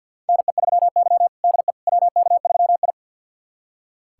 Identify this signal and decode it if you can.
Morse code, heard as a keyed tone, about 46 wpm, 700 Hz. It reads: NE3Y DE WK3I
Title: USA Callsign DE Callsign - 4 Characters - 50wpm